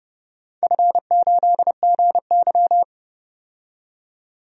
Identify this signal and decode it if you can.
Morse code, heard as a keyed tone, about 30 wpm, 700 Hz. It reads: F8GY
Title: DXCC-100 Call Signs - 4 Characters - 30wpm